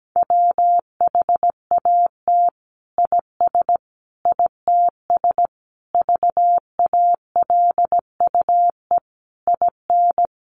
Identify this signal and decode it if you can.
Morse code, heard as a keyed tone, about 17 wpm, 700 Hz. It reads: WHAT IS ITS VALUE IN
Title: Sentences from Top 600 Words 17wpm